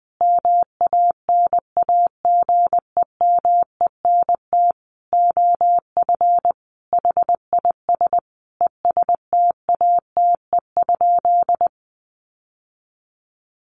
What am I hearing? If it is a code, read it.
Morse code, heard as a keyed tone, about 19 wpm, 700 Hz. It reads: MANAGEMENT OF HIS ESTATE?
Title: Sentences from Top 1200 Words 20wpm